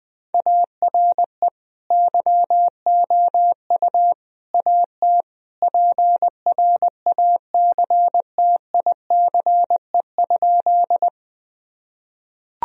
Morse code, heard as a keyed tone, about 20 wpm, 700 Hz. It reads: ARE YOU AT PRACTICE?
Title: Sentences from Top 1000 Words - Review Part 4 - 20wpm